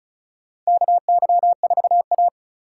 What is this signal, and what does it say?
Morse code, heard as a keyed tone, about 36 wpm, 700 Hz. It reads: KY4A
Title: USA Call Signs - 3x Character Spacing - 35wpm